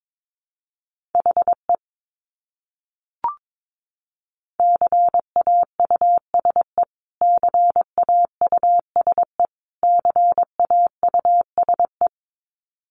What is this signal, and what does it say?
Morse code, heard as a keyed tone, about 22 wpm, 700 Hz. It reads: HE CAUSE CAUSE CAUSE
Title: Top 200 Words Repeated 22wpm